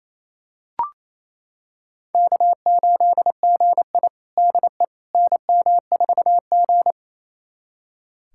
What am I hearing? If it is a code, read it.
Morse code, heard as a keyed tone, about 28 wpm, 700 Hz. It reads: K8GI DE NM4G
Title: USA Callsign DE Callsign - 4 Characters - 28wpm